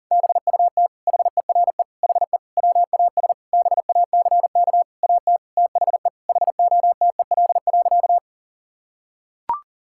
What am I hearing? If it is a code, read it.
Morse code, heard as a keyed tone, about 40 wpm, 700 Hz. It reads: BUT HERE HE WAS BACK AT THE HOTEL.
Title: Sentences from Top 1600 Words 40wpm